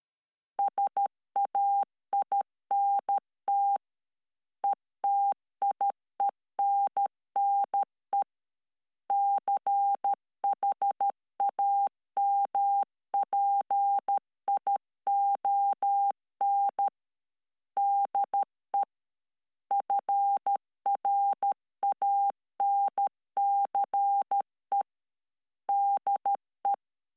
Morse code, heard as a keyed tone, about 13 wpm, 800 Hz. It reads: SAINT ETIENNE CHAMPION DE FRANCE DE